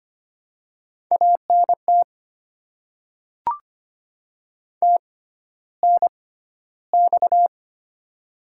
Morse code, heard as a keyed tone, about 25 wpm, spaced as for 13 wpm, 700 Hz. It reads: ANT TNX
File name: Top 20 QSO Elements - 6x Character Spacing - 25wpm